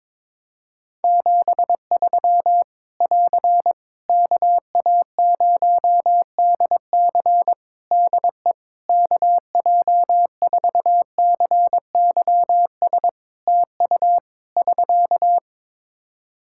Morse code, heard as a keyed tone, about 22 wpm, 700 Hz. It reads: 73 <AR> KA0DC DE KJ4CYS TU <SK>
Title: K1USN SST QSO Practice 22wpm